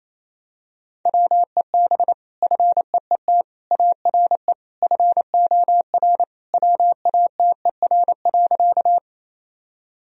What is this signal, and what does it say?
Morse code, heard as a keyed tone, about 28 wpm, 700 Hz. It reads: WEB FEET ARE FOR WATER.
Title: Sentences from Top 1000 Words - Review Part 4 - 28wpm